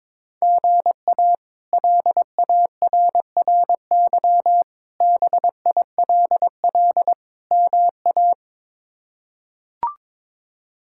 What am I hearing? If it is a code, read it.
Morse code, heard as a keyed tone, about 22 wpm, 700 Hz. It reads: GA LARRY BILL MA
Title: K1USN SST QSO Practice - 22wpm